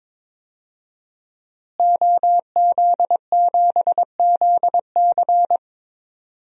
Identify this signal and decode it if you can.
Morse code, heard as a keyed tone, about 22 wpm, 700 Hz. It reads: OZ7ZC
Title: DXCC-100 Call Signs - 5 Characters - Repeated 2x - 22wpm